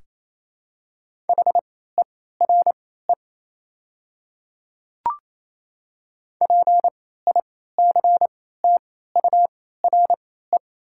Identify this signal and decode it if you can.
Morse code, heard as a keyed tone, about 28 wpm, spaced as for 16 wpm, 700 Hz. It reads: HERE PICTURE
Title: 3x Character Spacing - Top 200 Words - 28wpm